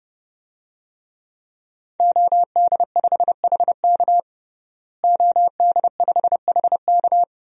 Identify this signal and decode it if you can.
Morse code, heard as a keyed tone, about 32 wpm, 700 Hz. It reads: OD5HK OD5HK
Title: DXCC-100 Call Signs - 5 Characters - Repeated 3x - 30wpm